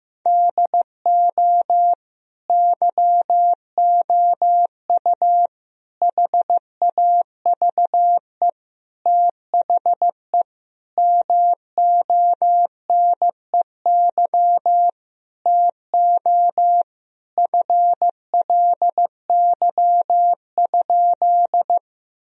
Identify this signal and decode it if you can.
Morse code, heard as a keyed tone, about 15 wpm, 700 Hz. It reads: DO YOU HAVE THE MONEY TO FLY?
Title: Sentences from Top 1000 Words - Review Part 2 - 15wpm